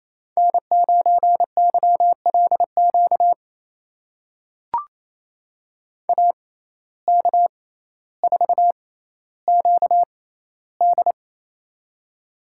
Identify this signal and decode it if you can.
Morse code, heard as a keyed tone, about 28 wpm, spaced as for 14 wpm, 700 Hz. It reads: N9YLQ AK4QD
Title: USA Call Signs - 5 Characters - 6x Character Spacing - 28wpm